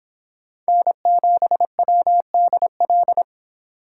Morse code, heard as a keyed tone, about 26 wpm, 700 Hz. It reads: N7WDL